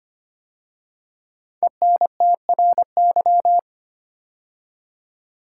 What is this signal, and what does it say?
Morse code, heard as a keyed tone, about 25 wpm, 700 Hz. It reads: ENTRY
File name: 5 Letter Words - First 3 Letters Given - 25wpm